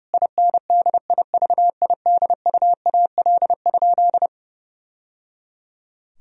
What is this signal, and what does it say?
Morse code, heard as a keyed tone, about 30 wpm, 700 Hz. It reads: INDIVIDUAL?
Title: Sentences from Top 1000 Words - Review Part 4 - 30wpm